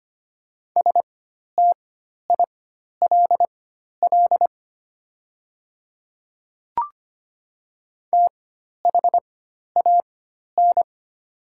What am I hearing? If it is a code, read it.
Morse code, heard as a keyed tone, about 25 wpm, spaced as for 12 wpm, 700 Hz. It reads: STILL THAN
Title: 4x Character Spacing - Top 300 Words - 25wpm